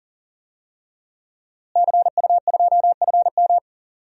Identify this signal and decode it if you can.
Morse code, heard as a keyed tone, about 40 wpm, 700 Hz. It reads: CU2FM
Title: DXCC-100 Call Signs - 5 Characters - Repeated 3x - 40wpm